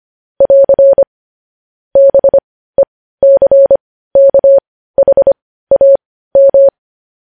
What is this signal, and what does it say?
Morse code, heard as a keyed tone, about 25 wpm, spaced as for 15 wpm, 550 Hz. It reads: <AR> BECKHAM